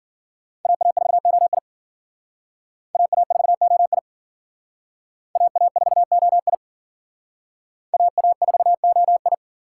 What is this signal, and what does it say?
Morse code, heard as a keyed tone, about 50 wpm, 700 Hz. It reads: AA4OI AA4OI AA4OI AA4OI